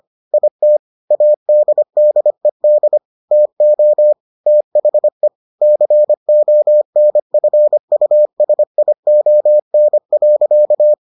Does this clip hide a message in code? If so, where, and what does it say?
Yes: Morse code from 0.3 to 10.9 s, heard as a keyed tone, about 25 wpm, 600 Hz. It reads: IT ADDED TO THE CONFUSION.